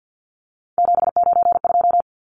Morse code, heard as a keyed tone, about 50 wpm, 700 Hz. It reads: 792
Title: Numbers - 3-Digit 50wpm